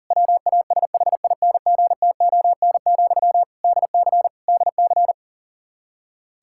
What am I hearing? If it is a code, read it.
Morse code, heard as a keyed tone, about 40 wpm, 700 Hz. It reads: WASHINGTON, DC DC
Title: Practice QSO - Exchange 1 - Beginning - 40wpm